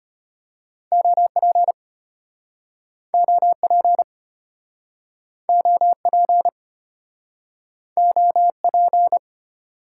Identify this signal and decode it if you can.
Morse code, heard as a keyed tone, about 36 wpm, 700 Hz. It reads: OP OP OP OP